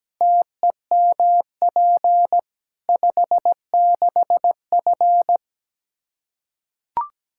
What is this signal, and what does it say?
Morse code, heard as a keyed tone, about 17 wpm, 700 Hz. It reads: TEMP 56F
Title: Practice QSO - Exchange 2 - Info - 17wpm